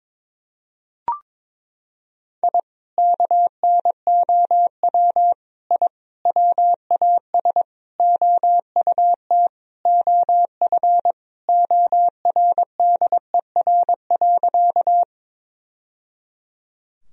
Morse code, heard as a keyed tone, about 22 wpm, 700 Hz. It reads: I KNOW I WAS OUT OF ORDER.
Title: Sentences from Top 1000 Words - Review Part 2 - 22wpm